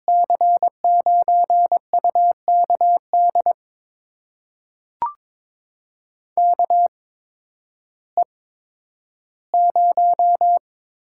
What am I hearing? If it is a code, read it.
Morse code, heard as a keyed tone, about 22 wpm, spaced as for 12 wpm, 700 Hz. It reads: C9UKD KE0
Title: USA Call Signs - 6 Characters - 8x Character Spacing - 22wpm